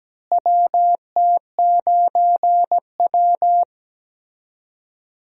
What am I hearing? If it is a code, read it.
Morse code, heard as a keyed tone, about 17 wpm, 700 Hz. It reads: WT9W
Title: USA Call Signs - 4 Characters - Repeated 3x - 17wpm